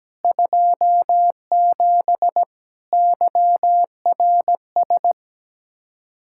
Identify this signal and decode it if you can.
Morse code, heard as a keyed tone, about 17 wpm, 700 Hz. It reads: 27 YRS
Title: Practice QSO - Exchange 3 - More Info - 17wpm